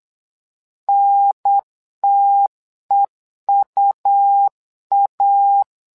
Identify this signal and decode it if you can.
Morse code, heard as a keyed tone, about 8 wpm, 800 Hz. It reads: NTEUA
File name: LEZIONI / 2 UHDN / 2 UHDN 8wpm2232